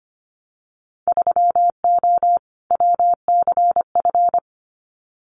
Morse code, heard as a keyed tone, about 25 wpm, 700 Hz. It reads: 3O WCF
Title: Winter Field Day Exchanges 25wpm